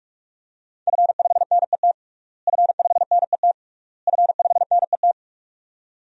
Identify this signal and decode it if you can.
Morse code, heard as a keyed tone, about 46 wpm, 700 Hz. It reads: F5NET F5NET F5NET
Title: DXCC-100 Call Signs - 5 Characters - Repeated 3x - 45wpm